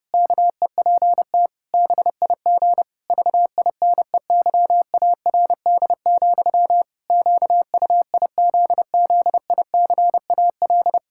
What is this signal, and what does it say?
Morse code, heard as a keyed tone, about 32 wpm, 700 Hz. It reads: KEPT BIG VINEYARD, QUIZZICAL
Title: Pangram Challenge v2 - 30wpm